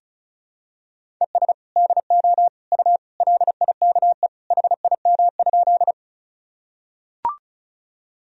Morse code, heard as a keyed tone, about 36 wpm, 700 Hz. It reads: ES DO U LIKE HIM?
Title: Sentences from Top 100 Words and Abbreviations 35wpm